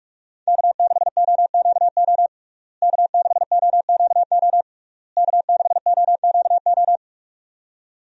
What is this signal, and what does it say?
Morse code, heard as a keyed tone, about 43 wpm, 700 Hz. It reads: K6OQO K6OQO K6OQO
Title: USA Call Signs - 5 Characters - Repeated 3x - 45wpm